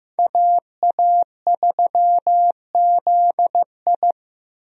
Morse code, heard as a keyed tone, about 15 wpm, 700 Hz. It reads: AA3ZI
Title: USA Call Signs - 5 Characters - Repeated 3x - 15wpm